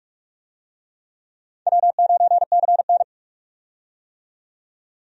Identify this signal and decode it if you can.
Morse code, heard as a keyed tone, about 46 wpm, 700 Hz. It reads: W9CN
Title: USA Call Signs - 4 Characters - 45wpm